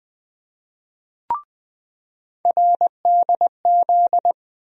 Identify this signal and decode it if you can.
Morse code, heard as a keyed tone, about 20 wpm, 700 Hz. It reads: RDZ
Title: USA Call Sign Postfixes - 3 Letters 20wpm